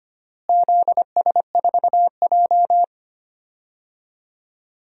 Morse code, heard as a keyed tone, about 25 wpm, 700 Hz. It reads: ZS4J
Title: DXCC-100 Call Signs - 4 Characters - Repeated 3x - 25wpm